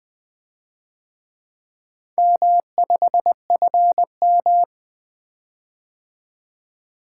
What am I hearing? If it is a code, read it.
Morse code, heard as a keyed tone, about 25 wpm, 700 Hz. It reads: M5FM
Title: Speed-Racing - Four Letter-Number Combinations - 20wpm